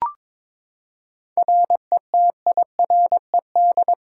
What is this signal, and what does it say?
Morse code, heard as a keyed tone, about 22 wpm, 700 Hz. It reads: RETIRED
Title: Practice QSO - Exchange 3 - More Info - 22wpm